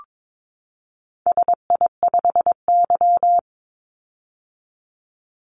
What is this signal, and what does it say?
Morse code, heard as a keyed tone, about 22 wpm, 700 Hz. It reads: SI5Y
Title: Call Signs - 22wpm